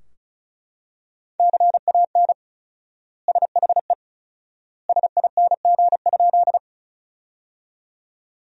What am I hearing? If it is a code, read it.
Morse code, heard as a keyed tone, about 36 wpm, 700 Hz. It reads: CAN SHE SING?
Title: 4x Word-Spacing - Sentences from Top 400 Words - 35wpm